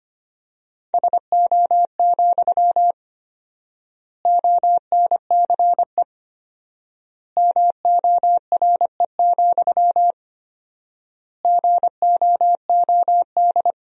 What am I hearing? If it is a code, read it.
Morse code, heard as a keyed tone, about 25 wpm, 700 Hz. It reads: SO, ONCE MORE, GOOD
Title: 4x Word-Spacing - Sentences from Top 300 Words - 25wpm